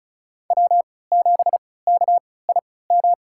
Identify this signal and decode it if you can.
Morse code, heard as a keyed tone, about 36 wpm, spaced as for 20 wpm, 700 Hz. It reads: W7KIM
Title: USA Call Signs - 5 Characters - 3x Character Spacing - 35wpm